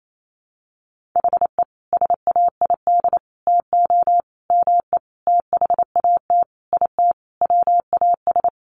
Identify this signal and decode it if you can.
Morse code, heard as a keyed tone, about 28 wpm, 700 Hz. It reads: HE SAID TO ME THAT IT WAS
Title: Sentences from Top 1500 Words 28wpm